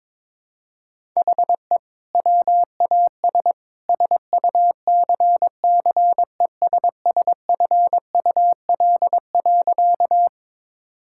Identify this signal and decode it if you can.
Morse code, heard as a keyed tone, about 22 wpm, 700 Hz. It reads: HE WAS SUCCESSFUL.
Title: Sentences from Top 1000 Words - Review Part 6 - 22wpm